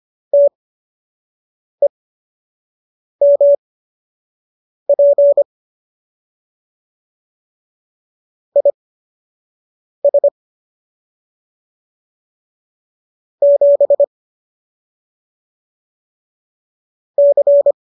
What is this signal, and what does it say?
Morse code, heard as a keyed tone, about 25 wpm, spaced as for 6 wpm, 570 Hz. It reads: TEMP IS 7 C